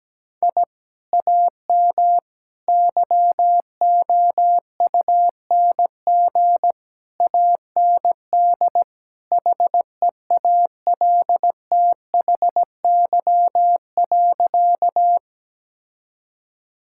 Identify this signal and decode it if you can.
Morse code, heard as a keyed tone, about 17 wpm, 700 Hz. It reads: I AM YOUNG AND HEALTHY.
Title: Sentences from Top 1000 Words - Review Part 5 - 17wpm